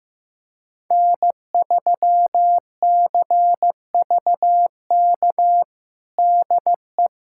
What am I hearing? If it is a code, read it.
Morse code, heard as a keyed tone, about 15 wpm, 700 Hz. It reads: N3CVK DE
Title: USA Callsign DE Callsign - 5 Characters - 15wpm